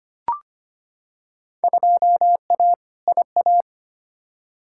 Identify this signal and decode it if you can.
Morse code, heard as a keyed tone, about 25 wpm, 700 Hz. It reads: 2A IA
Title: Field Day Exchanges - 25wpm